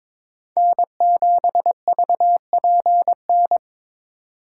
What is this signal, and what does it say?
Morse code, heard as a keyed tone, about 22 wpm, 700 Hz. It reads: N7VPN
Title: USA Call Signs - 6x Character Spacing - 22wpm